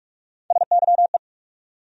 Morse code, heard as a keyed tone, about 46 wpm, 700 Hz. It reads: SYE
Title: USA Call Sign Postfixes - 3 Letters 45wpm